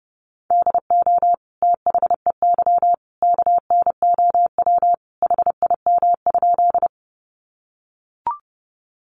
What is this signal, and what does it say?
Morse code, heard as a keyed tone, about 30 wpm, 700 Hz. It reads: DO THEY KNOW HIM?
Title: Sentences from Top 100 Words 30wpm